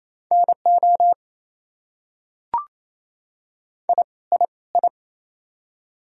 Morse code, heard as a keyed tone, about 28 wpm, 700 Hz. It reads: NO I I I
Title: Top 100 Words Repeated 28wpm